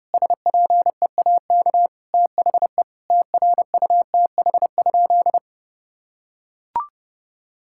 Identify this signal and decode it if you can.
Morse code, heard as a keyed tone, about 30 wpm, 700 Hz. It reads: SPEAK THE TRUTH?
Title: Sentences from Top 1600 Words 30wpm